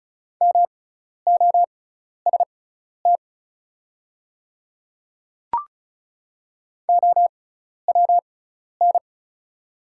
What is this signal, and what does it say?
Morse code, heard as a keyed tone, about 36 wpm, spaced as for 12 wpm, 700 Hz. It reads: MOST OWN
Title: 6x Character Spacing - Top 300 Words - 35wpm